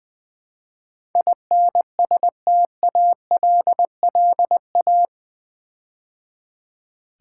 Morse code, heard as a keyed tone, about 20 wpm, 700 Hz. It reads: INSTALLA
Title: Build A Word - 4 Letters At A Time - 12-Letter Words - 20wpm